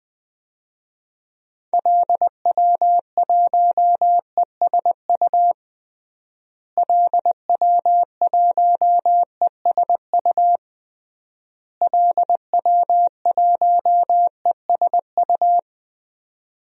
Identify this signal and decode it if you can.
Morse code, heard as a keyed tone, about 19 wpm, 700 Hz. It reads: LW1ESU LW1ESU LW1ESU
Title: DXCC-100 Call Signs - 6 Characters - Repeated 3x - 20wpm